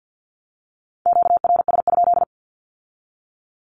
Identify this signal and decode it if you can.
Morse code, heard as a keyed tone, about 50 wpm, 700 Hz. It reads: QRS?
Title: Q-Codes with Spoken Meaning - 50wpm